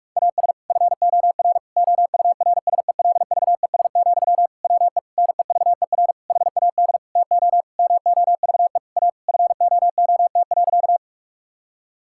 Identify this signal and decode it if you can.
Morse code, heard as a keyed tone, about 46 wpm, 700 Hz. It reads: AS FOR OURSELVES, WE NEVER HAD TO MOVE A FOOT.
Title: Sentences from Top 1600 Words 45wpm